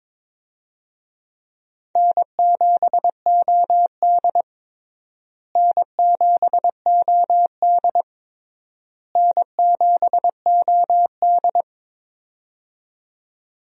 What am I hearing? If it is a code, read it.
Morse code, heard as a keyed tone, about 22 wpm, 700 Hz. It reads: N7OD N7OD N7OD
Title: USA Call Signs - 4 Characters - Repeated 3x - 22wpm